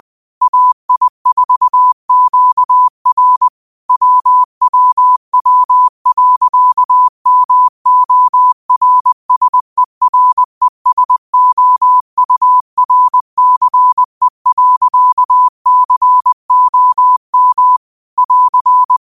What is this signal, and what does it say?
Morse code, heard as a keyed tone, about 20 wpm, 1000 Hz. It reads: AI4QR WWW.MORSERESOURCE.COM <AR>
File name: qotdfiles / 20wpm